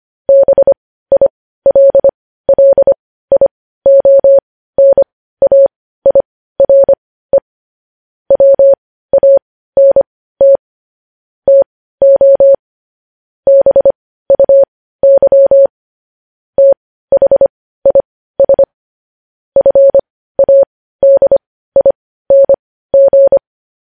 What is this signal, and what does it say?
Morse code, heard as a keyed tone, about 25 wpm, spaced as for 15 wpm, 550 Hz. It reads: BILLIONAIRE WANT TO BUY THIS FADING
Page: News Headlines in Morse Code at 15 WPM